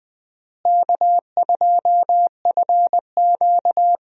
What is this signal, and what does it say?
Morse code, heard as a keyed tone, about 21 wpm, 700 Hz. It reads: K2FQ
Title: USA Call Signs - 4 Characters - Repeated 3x - 20wpm